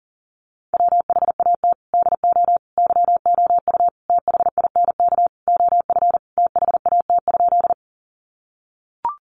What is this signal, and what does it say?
Morse code, heard as a keyed tone, about 40 wpm, 700 Hz. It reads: WHAT DO YOU THINK OF THAT?
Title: Sentences from Top 200 Words 40wpm